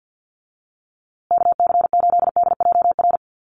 Morse code, heard as a keyed tone, about 50 wpm, 700 Hz. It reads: KC8DJR
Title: USA Call Signs Repeated 2x 50wpm